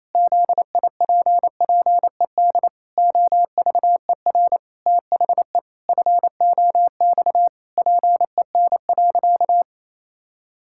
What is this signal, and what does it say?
Morse code, heard as a keyed tone, about 28 wpm, 700 Hz. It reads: ZIPPED OVER THE FOX PEN.
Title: Pangram Challenge v2 - 28wpm